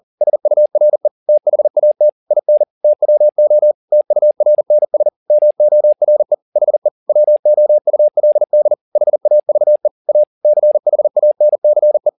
Morse code, heard as a keyed tone, about 40 wpm, 600 Hz. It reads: SURE THAT IN TWO TURNS MORE HE WOULD HAVE A CHANCE
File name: the old man and the sea 2 40wpm